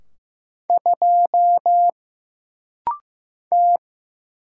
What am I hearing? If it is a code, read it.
Morse code, heard as a keyed tone, about 15 wpm, 700 Hz. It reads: 2 T
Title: Single Letter-Number Alt - Rapid-Fire 15wpm